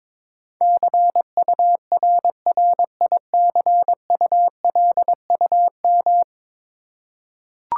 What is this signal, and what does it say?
Morse code, heard as a keyed tone, about 22 wpm, 700 Hz. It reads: CURRICULUM
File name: 10 Letter Words - First 3 Letters Given - 22wpm